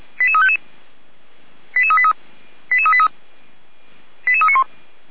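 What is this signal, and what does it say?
Five-tone selective call (ZVEI), heard as quick runs of five tones, tones of about 70 ms. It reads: 88355 88383 88383 88391
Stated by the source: ZVEI five-tone selective call received on an SDR, fire brigade Orlamünde (feuerwehr-orlamuende).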